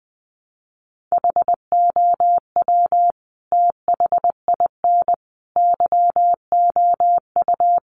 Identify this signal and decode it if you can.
Morse code, heard as a keyed tone, about 20 wpm, 700 Hz. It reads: HOW THIN YOU
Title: Sentences from Top 1200 Words 20wpm